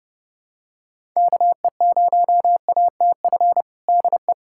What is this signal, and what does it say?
Morse code, heard as a keyed tone, about 32 wpm, 700 Hz. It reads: KE0ATF DE
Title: USA Callsign DE Callsign 30wpm